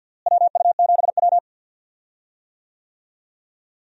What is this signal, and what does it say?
Morse code, heard as a keyed tone, about 51 wpm, 700 Hz. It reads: WU7W
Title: USA Call Signs - 4 Characters - Repeated 3x - 50wpm